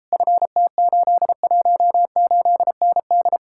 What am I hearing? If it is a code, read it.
Morse code, heard as a keyed tone, about 37 wpm, 700 Hz. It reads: FT818ND